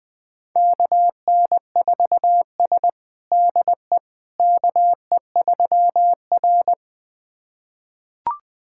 Morse code, heard as a keyed tone, about 20 wpm, 700 Hz. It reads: KN4S DE KE3R
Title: USA Callsign DE Callsign - 4 Characters - 20wpm